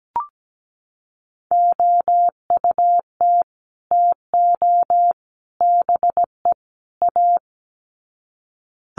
Morse code, heard as a keyed tone, about 17 wpm, 700 Hz. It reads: OUT TO BE A
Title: Sets of 4 Words 17wpm